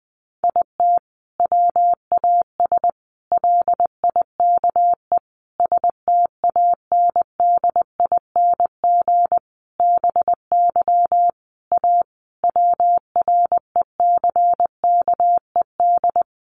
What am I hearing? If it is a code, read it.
Morse code, heard as a keyed tone, about 20 wpm, 700 Hz. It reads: IT WAS LIKE STANDING BY A WRECKED
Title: A Sound Of Thunder 20wpm